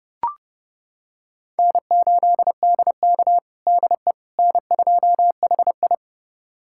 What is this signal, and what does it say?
Morse code, heard as a keyed tone, about 30 wpm, 700 Hz. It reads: N8DK DE N2HI
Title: USA Callsign DE Callsign - 4 Characters - 30wpm